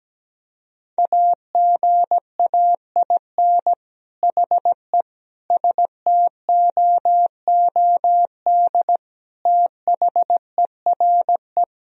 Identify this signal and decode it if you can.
Morse code, heard as a keyed tone, about 17 wpm, 700 Hz. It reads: AGAIN HE STOOD THERE
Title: Sentences from Top 1000 Words - Review Part 3 - 17wpm